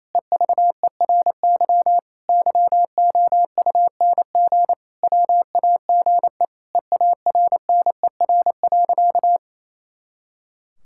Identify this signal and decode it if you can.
Morse code, heard as a keyed tone, about 28 wpm, 700 Hz. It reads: EVERY YOUNG WAGE EARNER.
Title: Pangram Challenge v2 - 28wpm